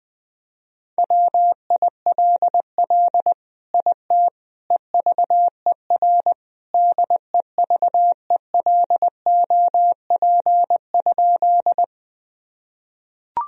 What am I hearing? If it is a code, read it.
Morse code, heard as a keyed tone, about 20 wpm, 700 Hz. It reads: WILL IT EVER DEVELOP?
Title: Sentences from Top 1000 Words - Review Part 2 - 20wpm